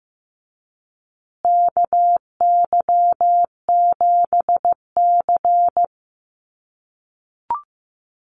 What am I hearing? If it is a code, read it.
Morse code, heard as a keyed tone, about 15 wpm, 700 Hz. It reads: KY7C
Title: USA Call Signs Repeated 3x 15wpm